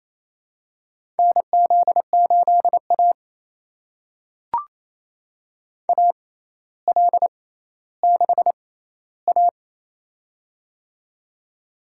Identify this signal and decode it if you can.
Morse code, heard as a keyed tone, about 28 wpm, spaced as for 14 wpm, 700 Hz. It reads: NZ8A AL6A
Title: USA Call Signs - 4 Characters - 6x Character Spacing - 28wpm